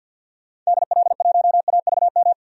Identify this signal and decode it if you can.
Morse code, heard as a keyed tone, about 52 wpm, 700 Hz. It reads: DL1AVM
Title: DXCC-100 Call Signs - 6 Characters - 50wpm